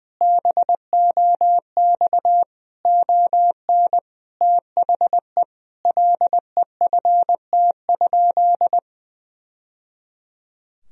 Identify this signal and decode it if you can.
Morse code, heard as a keyed tone, about 20 wpm, 700 Hz. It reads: BOX ON THE LEFT?
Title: Sentences from Top 1000 Words - Review Part 2 - 20wpm